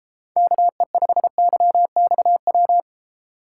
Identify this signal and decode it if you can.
Morse code, heard as a keyed tone, about 33 wpm, 700 Hz. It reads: KE5YXW